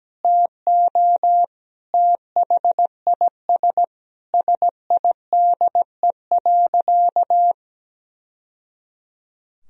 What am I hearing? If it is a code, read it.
Morse code, heard as a keyed tone, about 17 wpm, 700 Hz. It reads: TO THIS SIDE.
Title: Sentences from Top 1000 Words - Review Part 1 - 17wpm